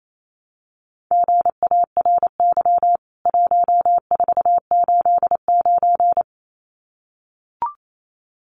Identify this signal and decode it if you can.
Morse code, heard as a keyed tone, about 28 wpm, 700 Hz. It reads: GARY 1489
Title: CWT Exchange 28wpm